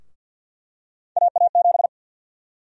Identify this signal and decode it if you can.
Morse code, heard as a keyed tone, about 52 wpm, 700 Hz. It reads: AA7
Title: USA Call Sign Prefixes - Repeated 2x - 50wpm